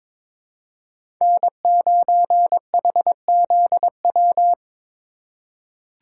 Morse code, heard as a keyed tone, about 21 wpm, 700 Hz. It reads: N9HZW
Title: USA Call Signs - 5 Characters - 4x Character Spacing - 22wpm